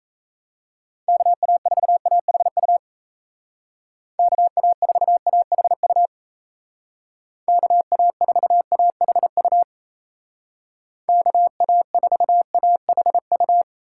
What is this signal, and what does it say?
Morse code, heard as a keyed tone, about 37 wpm, 700 Hz. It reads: KA4AHU KA4AHU KA4AHU KA4AHU